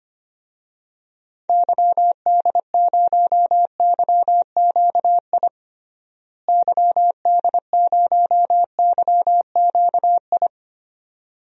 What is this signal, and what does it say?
Morse code, heard as a keyed tone, about 25 wpm, 700 Hz. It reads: YD0YQI YD0YQI
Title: DXCC-100 Call Signs - 6 Characters - Repeated 2x - 25wpm